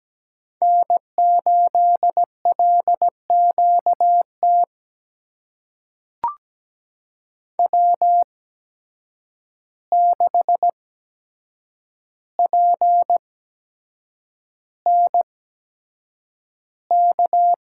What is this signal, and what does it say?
Morse code, heard as a keyed tone, about 17 wpm, spaced as for 7 wpm, 700 Hz. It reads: N8LQT W6PNK
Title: USA Call Signs - 5 Characters - 8x Character Spacing - 17wpm